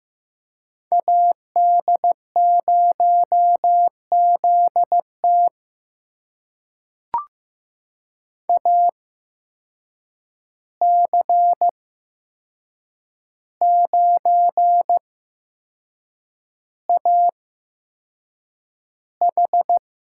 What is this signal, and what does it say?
Morse code, heard as a keyed tone, about 15 wpm, spaced as for 6 wpm, 700 Hz. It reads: AD0ZT AC9AH
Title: USA Call Signs - 5 Characters - 8x Character Spacing - 15wpm